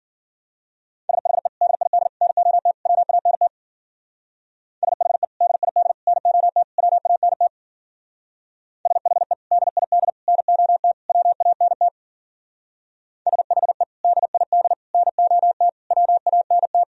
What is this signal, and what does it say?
Morse code, heard as a keyed tone, about 50 wpm, 700 Hz. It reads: SHE DID NOT WANT SHE DID NOT WANT SHE DID NOT WANT SHE DID NOT WANT
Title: Speed-Racing - Sets of 4 Words 40wpm